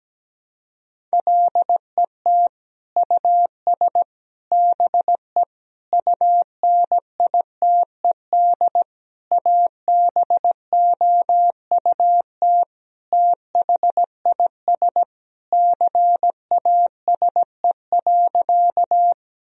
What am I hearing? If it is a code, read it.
Morse code, heard as a keyed tone, about 17 wpm, 700 Hz. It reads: LET US BE UNITED ABOUT THIS CASE.
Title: Sentences from Top 1000 Words - Review Part 4 - 17wpm